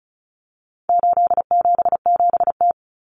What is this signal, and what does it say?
Morse code, heard as a keyed tone, about 34 wpm, 700 Hz. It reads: 877T
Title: Common Cut Numbers - 4-Digit 35wpm